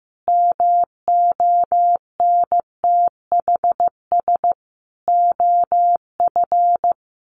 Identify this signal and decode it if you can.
Morse code, heard as a keyed tone, about 15 wpm, 700 Hz. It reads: MONTHS OF
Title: Sets of 5 Words Encore2 15wpm